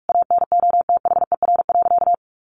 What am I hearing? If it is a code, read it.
Morse code, heard as a keyed tone, about 44 wpm, 700 Hz. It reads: ANOTHER.
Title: Sentences from Top 1400 Words 45wpm